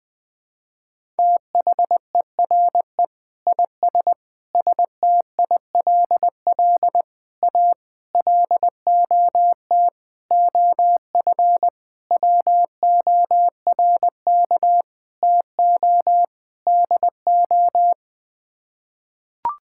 Morse code, heard as a keyed tone, about 21 wpm, 700 Hz. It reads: THERE IS STILL A LOT OF WORK TO DO
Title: Sets of 9 Words - 20wpm